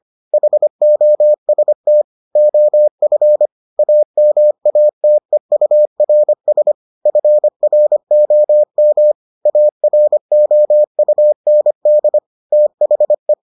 Morse code, heard as a keyed tone, about 25 wpm, 600 Hz. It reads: HOST OF AMATEURS FROM AROUND THE